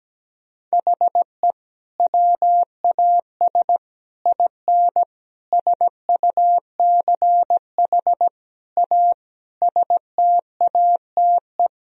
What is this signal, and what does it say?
Morse code, heard as a keyed tone, about 17 wpm, 700 Hz. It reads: HE WAS IN SUCH A STATE
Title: Sentences from Top 1000 Words - Review Part 2 - 17wpm